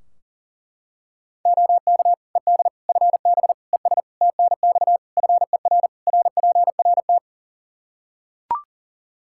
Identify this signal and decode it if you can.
Morse code, heard as a keyed tone, about 40 wpm, 700 Hz. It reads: OK ED FB ES TNX FER RPRT
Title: K1USN SST QSO Practice 40wpm